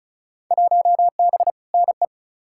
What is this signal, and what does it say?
Morse code, heard as a keyed tone, about 35 wpm, 700 Hz. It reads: 1B NE
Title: Field Day Exchanges Encore - 35wpm